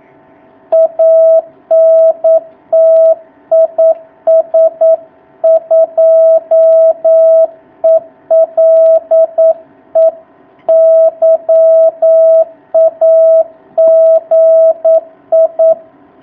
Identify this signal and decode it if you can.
Morse code, heard as a keyed tone, about 9 wpm, 650 Hz. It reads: ANT IS 2ELE YAGI
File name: jh8cbh / 2019cw